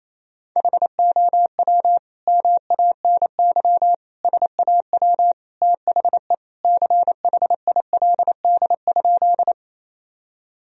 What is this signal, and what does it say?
Morse code, heard as a keyed tone, about 28 wpm, 700 Hz. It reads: HOW MANY SAW THE CHILD?
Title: Sentences from Top 1000 Words - Review Part 1 - 28wpm